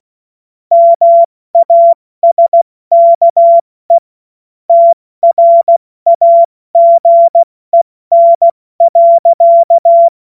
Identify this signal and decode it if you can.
Morse code, heard as a keyed tone, about 16 wpm, 680 Hz. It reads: MASKE TRAGEN.
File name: Morsekurs / Witze / Witz 13. Juli 2021, 70 F80